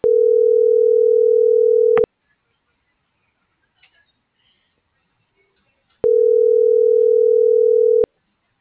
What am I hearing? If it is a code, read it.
Ringback tone (North America), 440 + 480 Hz.
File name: CENTRAL GOMSA / out alan.escudero